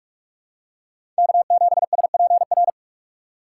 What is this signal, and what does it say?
Morse code, heard as a keyed tone, about 43 wpm, 700 Hz. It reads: K7SPR
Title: USA Call Signs - 4x Character Spacing - 45wpm